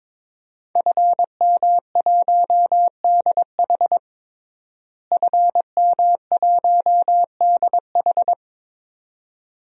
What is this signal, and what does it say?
Morse code, heard as a keyed tone, about 22 wpm, 700 Hz. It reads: FM1DH FM1DH
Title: DXCC-100 Call Signs - 5 Characters - Repeated 2x - 22wpm